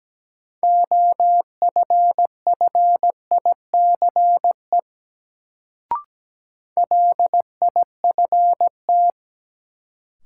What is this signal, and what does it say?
Morse code, heard as a keyed tone, about 17 wpm, 700 Hz. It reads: OFFICE LIFT
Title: Sentences from Top 1000 Words - New Word Drill - 17wpm